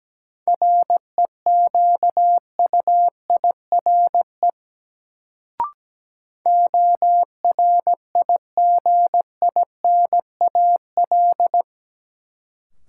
Morse code, heard as a keyed tone, about 17 wpm, 700 Hz. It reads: REQUIRE ORIGINAL
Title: Sentences from Top 900 Words - New Word Drill - 17wpm